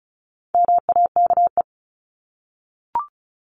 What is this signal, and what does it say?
Morse code, heard as a keyed tone, about 34 wpm, 700 Hz. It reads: MAKE
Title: Top 500 Words 35wpm